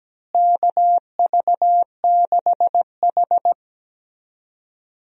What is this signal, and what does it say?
Morse code, heard as a keyed tone, about 17 wpm, 700 Hz. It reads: KV6H
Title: USA Call Signs - 4 Characters - Repeated 3x - 17wpm